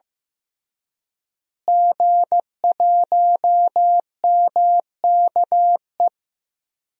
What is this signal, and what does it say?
Morse code, heard as a keyed tone, about 15 wpm, 700 Hz. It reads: G1MKE
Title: DXCC-100 Call Signs - 5 Characters - Repeated 3x - 15wpm